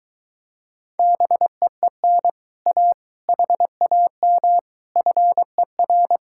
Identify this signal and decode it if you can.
Morse code, heard as a keyed tone, about 23 wpm, 700 Hz. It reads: BEEN A HAM FER